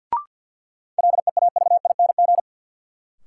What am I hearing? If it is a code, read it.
Morse code, heard as a keyed tone, about 53 wpm, 700 Hz. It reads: LEAVING
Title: Sentences from Top 1000 Words - Review Part 4 - New Word Drill - 50wpm